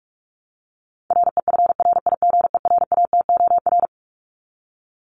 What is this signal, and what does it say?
Morse code, heard as a keyed tone, about 44 wpm, 700 Hz. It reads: REFRIGERATOR
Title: Top 1501-1750 Words 45wpm